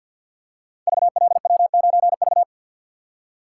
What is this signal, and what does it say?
Morse code, heard as a keyed tone, about 49 wpm, 700 Hz. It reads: VLW9V
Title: Code Groups - 5 Letter:Numbers - 50wpm